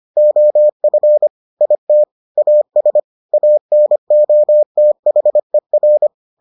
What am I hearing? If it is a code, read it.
Morse code, heard as a keyed tone, about 25 wpm, 600 Hz. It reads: OF IT AS ANOTHER